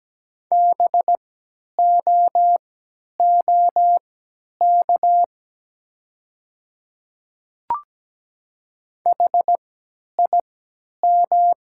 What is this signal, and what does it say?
Morse code, heard as a keyed tone, about 17 wpm, spaced as for 10 wpm, 700 Hz. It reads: BOOK HIM
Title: 3x Character Spacing - Top 300 Words - 17wpm